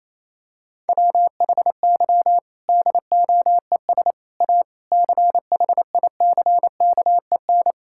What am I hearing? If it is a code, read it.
Morse code, heard as a keyed tone, about 28 wpm, 700 Hz. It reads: WHY DOES A CHICKEN
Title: Sentences from Top 1000 Words - Review Part 5 - 28wpm